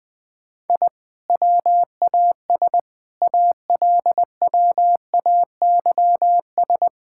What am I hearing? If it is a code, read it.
Morse code, heard as a keyed tone, about 20 wpm, 700 Hz. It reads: I WAS ALWAYS
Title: Sentences from Top 1000 Words - Review Part 6 - 20wpm